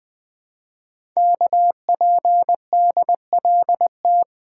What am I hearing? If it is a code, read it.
Morse code, heard as a keyed tone, about 20 wpm, 700 Hz. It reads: KPDLT
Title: Code Groups - 5 Letters - 20wpm